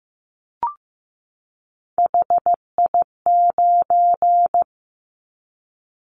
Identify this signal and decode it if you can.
Morse code, heard as a keyed tone, about 15 wpm, 700 Hz. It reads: HI9
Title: Callsign Prefixes for Top 100 DXCC - 15wpm